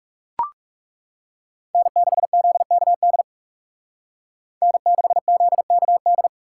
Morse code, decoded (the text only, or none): N6ZKD N6ZKD